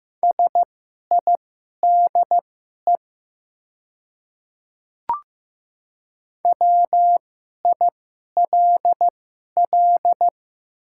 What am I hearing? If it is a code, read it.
Morse code, heard as a keyed tone, about 15 wpm, spaced as for 11 wpm, 690 Hz. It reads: SIDE WILL
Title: 2x Character Spacing - Top 300 Words - 15wpm